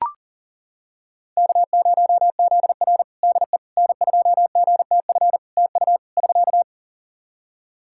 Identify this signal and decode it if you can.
Morse code, heard as a keyed tone, about 40 wpm, 700 Hz. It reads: K0ZR DE N2GTF TU <SK>
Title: Practice QSO - Exchange 4 - Ending - 40wpm